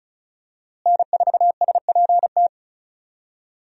Morse code, heard as a keyed tone, about 34 wpm, 700 Hz. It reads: N4SPT
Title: USA Call Signs - 5 Characters - Repeated 2x - 35wpm